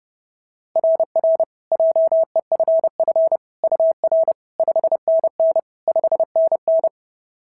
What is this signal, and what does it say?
Morse code, heard as a keyed tone, about 26 wpm, 650 Hz. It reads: RR JEFF UR 5NN 5NN
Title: SOTA QSO Practice 30wpm